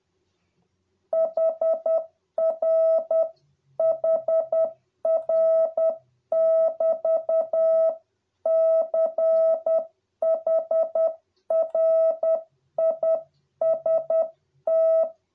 Morse code, heard as a keyed tone, about 10 wpm, 650 Hz. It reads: HR HR <BT> CHRIST